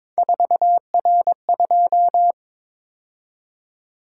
Morse code, heard as a keyed tone, about 22 wpm, 700 Hz. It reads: 4R2
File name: practice / 22wpm / Three Letter-Number Combinations - 22wpm